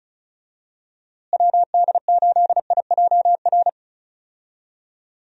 Morse code, heard as a keyed tone, about 34 wpm, 700 Hz. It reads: WD8IJR
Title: USA Call Signs - 6 Characters - Repeated 3x - 35wpm